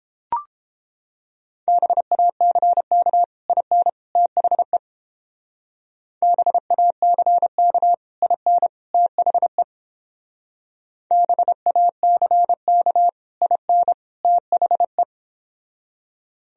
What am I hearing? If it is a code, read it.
Morse code, heard as a keyed tone, about 31 wpm, 700 Hz. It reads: BACK IN THE BACK IN THE BACK IN THE